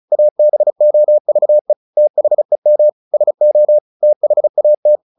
Morse code, heard as a keyed tone, about 34 wpm, 600 Hz. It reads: ABOVE THEM SO THAT
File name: the old man and the sea 2 35wpm